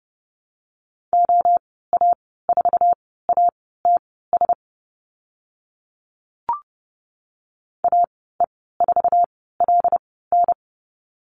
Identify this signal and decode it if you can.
Morse code, heard as a keyed tone, about 30 wpm, spaced as for 17 wpm, 700 Hz. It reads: OA4ATS AE4LN
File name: Lesson 15g - Number 5 - Call Signs 30wpm